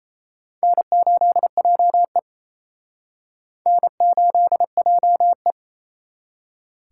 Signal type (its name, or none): Morse code